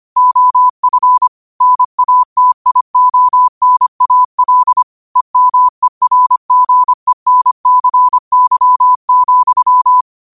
Morse code, heard as a keyed tone, about 25 wpm, 1000 Hz. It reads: OF NATIONAL EMERGENCY,